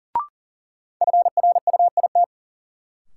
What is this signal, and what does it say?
Morse code, heard as a keyed tone, about 37 wpm, 700 Hz. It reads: FRUIT
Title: Sentences from Top 1300 Words - New Word Drill - 40wpm